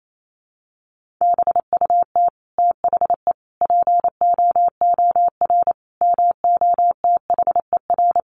Morse code, heard as a keyed tone, about 28 wpm, 700 Hz. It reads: BUT THE POOR MOTHER
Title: Sentences from Top 1000 Words 28wpm